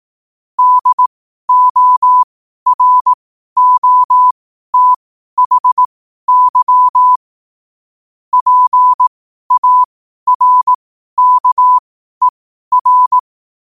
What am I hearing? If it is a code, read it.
Morse code, heard as a keyed tone, about 18 wpm, spaced as for 12 wpm, 1000 Hz. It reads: DOROTHY PARKER
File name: qotdfiles / 12wpm